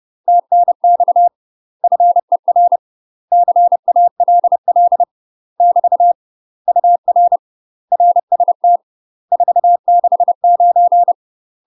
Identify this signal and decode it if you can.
Morse code, heard as a keyed tone, about 30 wpm, 700 Hz. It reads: TNX FER CALL <BT> UR RST 469
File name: UK MASTER QSOs TRAINING/30WPM